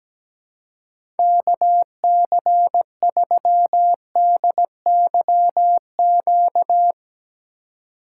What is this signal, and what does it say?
Morse code, heard as a keyed tone, about 17 wpm, 700 Hz. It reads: KC3DYQ
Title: USA Call Signs - 6 Characters - 2x Character Spacing - 17wpm